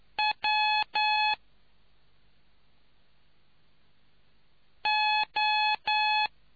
Morse code, heard as a keyed tone, about 9 wpm, spaced as for 2 wpm, 850 Hz. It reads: WO